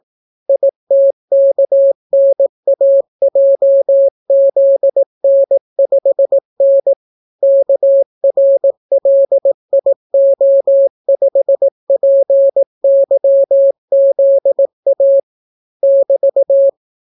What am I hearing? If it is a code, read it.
Morse code, heard as a keyed tone, about 18 wpm, 550 Hz. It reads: ITKNAJZN5N KRLIO5PYZA <BT>